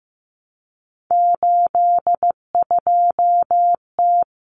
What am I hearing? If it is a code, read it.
Morse code, heard as a keyed tone, about 15 wpm, 700 Hz. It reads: 82T
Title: Common Cut Numbers - 3-Digit 15wpm